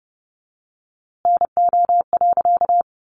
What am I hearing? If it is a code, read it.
Morse code, heard as a keyed tone, about 29 wpm, 700 Hz. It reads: NO.
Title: A Sound Of Thunder 30wpm